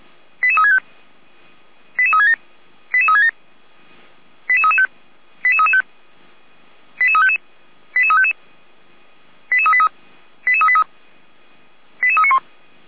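Five-tone selective call (ZVEI), heard as quick runs of five tones, tones of about 80 ms. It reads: 88365 88367 88367 88335 88335 88355 88355 88383 88383 88391
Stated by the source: ZVEI five-tone call 88073 received on SDR.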